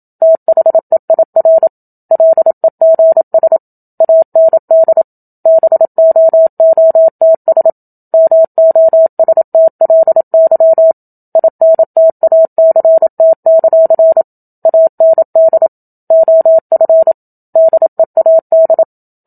Morse code, heard as a keyed tone, about 27 wpm, 650 Hz. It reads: THEIR LEGS AND BOOTS MOSTLY INTACT; AND OF DEAD
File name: Book Two / 01-Chapter 1 - 27 WPM